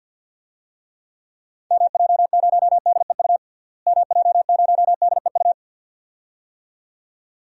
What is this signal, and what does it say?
Morse code, heard as a keyed tone, about 48 wpm, 700 Hz. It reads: MJ0DEU MJ0DEU
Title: DXCC-100 Call Signs - 6 Characters - Repeated 2x - 50wpm